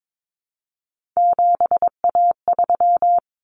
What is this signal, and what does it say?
Morse code, heard as a keyed tone, about 22 wpm, 700 Hz. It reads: 7A3
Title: Common Cut Numbers - 3-Digit 22wpm